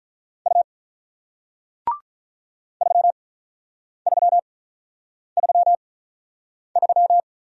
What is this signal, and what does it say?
Morse code, heard as a keyed tone, about 48 wpm, spaced as for 9 wpm, 700 Hz. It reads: U 3333